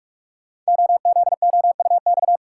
Morse code, heard as a keyed tone, about 44 wpm, 700 Hz. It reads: OZOUX
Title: Code Groups - 5 Letter:Numbers - 45wpm